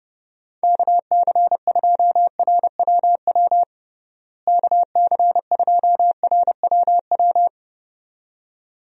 Morse code, heard as a keyed tone, about 30 wpm, 700 Hz. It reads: KC2RWW KC2RWW
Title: USA Call Signs - 6 Characters - Repeated 2x - 30wpm